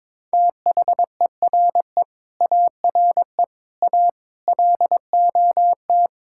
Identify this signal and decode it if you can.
Morse code, heard as a keyed tone, about 22 wpm, 700 Hz. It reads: THERE ARE A LOT
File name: Sentences with 9-Word Phrases - 22wpm